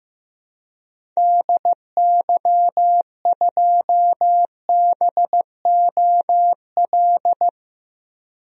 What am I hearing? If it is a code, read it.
Morse code, heard as a keyed tone, about 15 wpm, 700 Hz. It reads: DY2BOL
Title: DXCC-100 Call Signs - 6 Characters - 15wpm